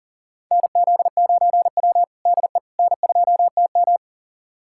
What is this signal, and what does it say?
Morse code, heard as a keyed tone, about 40 wpm, 700 Hz. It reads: NZ9W DE N2TM
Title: USA Callsign DE Callsign - 4 Characters - 40wpm